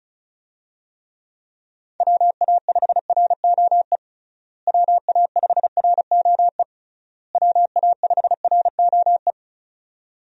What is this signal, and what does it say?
Morse code, heard as a keyed tone, about 36 wpm, 700 Hz. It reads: WA5ROE WA5ROE WA5ROE
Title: USA Call Signs - 6 Characters - Repeated 3x - 35wpm